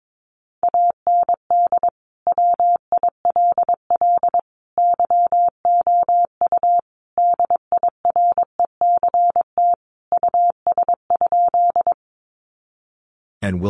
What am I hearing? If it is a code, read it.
Morse code, heard as a keyed tone, about 22 wpm, 700 Hz. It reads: AND WILL YOU DIRECT US?
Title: Sentences from Top 400 Words 22wpm